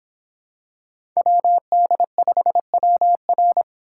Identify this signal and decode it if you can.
Morse code, heard as a keyed tone, about 24 wpm, 700 Hz. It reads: WD5WR